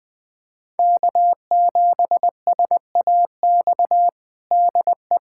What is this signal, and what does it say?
Morse code, heard as a keyed tone, about 20 wpm, 700 Hz. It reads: K7SAX DE
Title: USA Callsign DE Callsign - 5 Characters - 20wpm